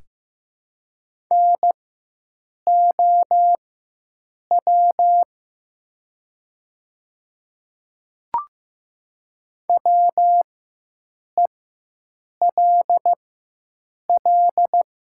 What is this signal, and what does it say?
Morse code, heard as a keyed tone, about 15 wpm, spaced as for 7 wpm, 700 Hz. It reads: NOW WELL
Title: 4x Character Spacing - Top 300 Words - 15wpm